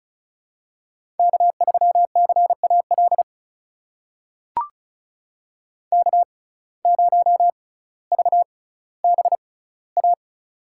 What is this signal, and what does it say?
Morse code, heard as a keyed tone, about 36 wpm, spaced as for 18 wpm, 700 Hz. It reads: K3CAL K0VBA
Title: USA Call Signs - 5 Characters - 6x Character Spacing - 35wpm